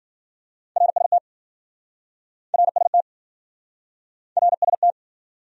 Morse code, heard as a keyed tone, about 52 wpm, 700 Hz. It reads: RST RST RST